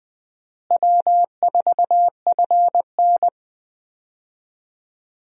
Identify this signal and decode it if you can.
Morse code, heard as a keyed tone, about 20 wpm, 700 Hz. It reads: W4FN
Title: USA Call Signs - 4 Characters - Repeated 2x - 20wpm